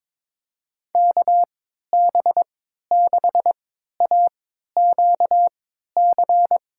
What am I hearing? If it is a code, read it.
Morse code, heard as a keyed tone, about 22 wpm, spaced as for 13 wpm, 700 Hz. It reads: KB6AQC
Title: USA Call Signs - 3x Character Spacing - 22wpm